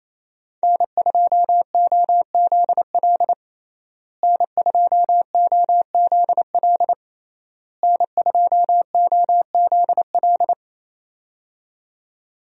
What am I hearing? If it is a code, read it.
Morse code, heard as a keyed tone, about 28 wpm, 700 Hz. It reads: N2OZL N2OZL N2OZL
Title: USA Call Signs - 5 Characters - Repeated 3x - 28wpm